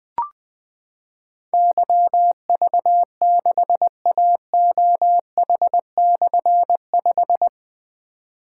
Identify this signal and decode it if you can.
Morse code, heard as a keyed tone, about 20 wpm, 700 Hz. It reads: YV6AOH/5
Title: Difficult Call Signs 20wpm